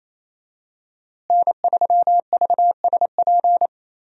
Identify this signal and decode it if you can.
Morse code, heard as a keyed tone, about 28 wpm, 700 Hz. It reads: N3VSP
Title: USA Call Signs - 5 Characters - Repeated 2x - 28wpm